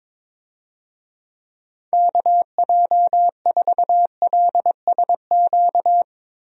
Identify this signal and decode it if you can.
Morse code, heard as a keyed tone, about 22 wpm, 700 Hz. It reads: KJ4LSQ
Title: USA Call Signs - 6 Characters - Repeated 2x - 22wpm